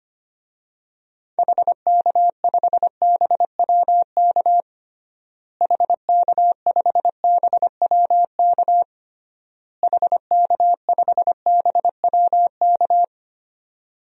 Morse code, heard as a keyed tone, about 25 wpm, 700 Hz. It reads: HK5BWK HK5BWK HK5BWK
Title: DXCC-100 Call Signs - 6 Characters - Repeated 3x - 25wpm